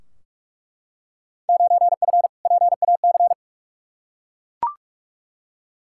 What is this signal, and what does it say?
Morse code, heard as a keyed tone, about 46 wpm, 700 Hz. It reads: 9F PAC
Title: Field Day Exchanges Encore - 45wpm